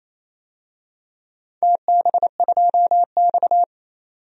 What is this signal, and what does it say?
Morse code, heard as a keyed tone, about 27 wpm, 700 Hz. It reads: TB2X
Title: DXCC-100 Call Signs - 4 Characters - 28wpm